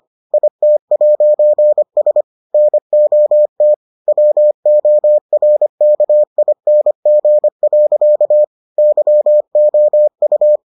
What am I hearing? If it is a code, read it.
Morse code, heard as a keyed tone, about 25 wpm, 600 Hz. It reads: IT'S NOT WORKING. YOU